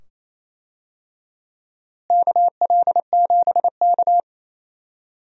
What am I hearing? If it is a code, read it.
Morse code, heard as a keyed tone, about 28 wpm, 700 Hz. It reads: KL7K
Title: DXCC-100 Call Signs - 4 Characters - Repeated 2x - 28wpm